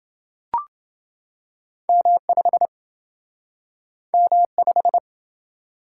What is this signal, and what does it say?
Morse code, heard as a keyed tone, about 28 wpm, 700 Hz. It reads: M5 M5